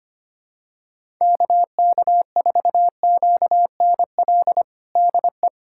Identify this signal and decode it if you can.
Morse code, heard as a keyed tone, about 25 wpm, 700 Hz. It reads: KK4QNL DE
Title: USA Callsign DE Callsign - 6 Characters - 25wpm